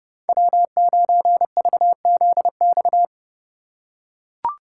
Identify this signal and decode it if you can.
Morse code, heard as a keyed tone, about 30 wpm, 700 Hz. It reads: W9VZX
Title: USA Call Signs - 4x Character Spacing - 30wpm